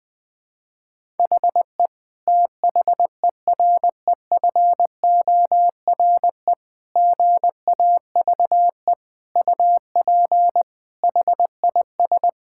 Morse code, heard as a keyed tone, about 20 wpm, 700 Hz. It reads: HE THEREFORE GAVE UP HIS
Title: Sentences from Top 1000 Words - Review Part 5 - 20wpm